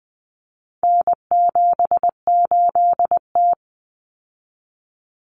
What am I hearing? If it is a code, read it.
Morse code, heard as a keyed tone, about 20 wpm, 700 Hz. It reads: N78T
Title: Common Cut Numbers - 4-Digit 20wpm